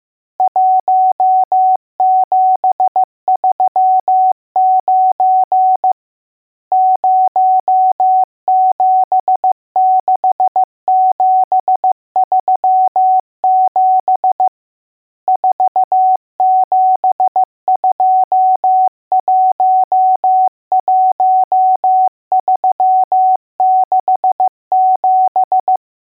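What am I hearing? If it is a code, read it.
Morse code, heard as a keyed tone, about 15 wpm, 750 Hz. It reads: 1739 076737 47211367
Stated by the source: Morse code lesson: test on all digits.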